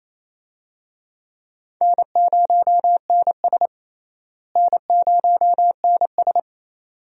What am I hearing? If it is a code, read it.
Morse code, heard as a keyed tone, about 28 wpm, 700 Hz. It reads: N0NS N0NS
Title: USA Call Signs - 4 Characters - Repeated 2x - 28wpm